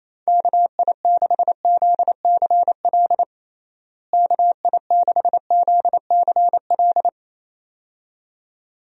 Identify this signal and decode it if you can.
Morse code, heard as a keyed tone, about 28 wpm, 700 Hz. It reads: KI6ZCL KI6ZCL
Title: USA Call Signs - 6 Characters - Repeated 2x - 28wpm